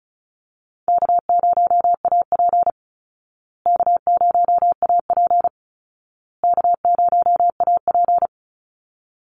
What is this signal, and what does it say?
Morse code, heard as a keyed tone, about 34 wpm, 700 Hz. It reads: K0AP K0AP K0AP
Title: Call Signs Repeated 2x - 35wpm